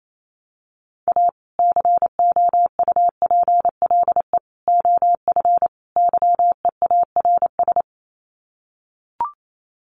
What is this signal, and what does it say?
Morse code, heard as a keyed tone, about 28 wpm, 700 Hz. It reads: A COUPLE OF YEARS
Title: Sets of 4 Words 28wpm